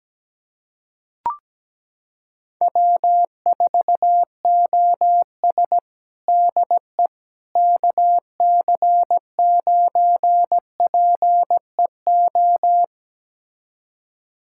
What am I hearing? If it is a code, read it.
Morse code, heard as a keyed tone, about 17 wpm, 700 Hz. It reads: W4OS DE KC9PEO
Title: USA Callsign DE Callsign 17wpm